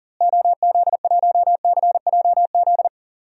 Morse code, heard as a keyed tone, about 37 wpm, 700 Hz. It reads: OZ1CJZ
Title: DXCC-100 Call Signs - 6 Characters - Repeated 3x - 40wpm